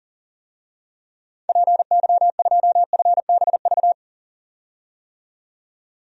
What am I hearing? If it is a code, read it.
Morse code, heard as a keyed tone, about 40 wpm, 700 Hz. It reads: PY2FBV
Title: DXCC-100 Call Signs - 6 Characters - 40wpm